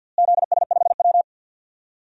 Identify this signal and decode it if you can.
Morse code, heard as a keyed tone, about 51 wpm, 700 Hz. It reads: ZS5W
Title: DXCC-100 Call Signs - 4 Characters - Repeated 2x - 50wpm